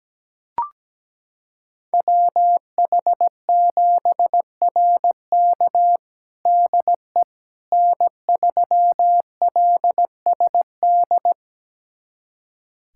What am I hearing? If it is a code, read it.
Morse code, heard as a keyed tone, about 17 wpm, 700 Hz. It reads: WH7RK DE N3LSD
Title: USA Callsign DE Callsign - 5 Characters - 17wpm